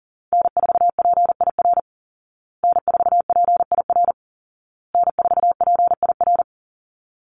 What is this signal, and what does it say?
Morse code, heard as a keyed tone, about 40 wpm, 700 Hz. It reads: N4PIR N4PIR N4PIR
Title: Call Signs Repeated 2x - 40wpm